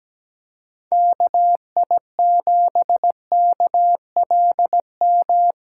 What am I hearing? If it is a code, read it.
Morse code, heard as a keyed tone, about 17 wpm, 700 Hz. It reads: KI7KLM
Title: USA Callsign DE Callsign - 6 Characters - 17wpm